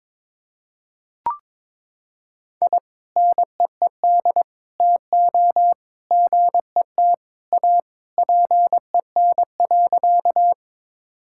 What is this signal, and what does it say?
Morse code, heard as a keyed tone, about 22 wpm, 700 Hz. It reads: I NEED TO GET A PEN.
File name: practice / 22wpm / Sentences with 2-Word Phrases - 22wpm